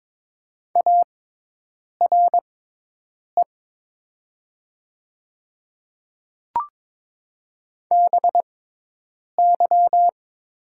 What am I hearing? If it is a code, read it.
Morse code, heard as a keyed tone, about 22 wpm, spaced as for 8 wpm, 700 Hz. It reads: ARE BY
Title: 6x Character Spacing - Top 300 Words - 22wpm